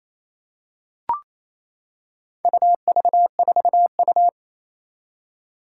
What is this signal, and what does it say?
Morse code, heard as a keyed tone, about 29 wpm, 700 Hz. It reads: UV4U
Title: Four Challenging Letter-Number Combinations 28wpm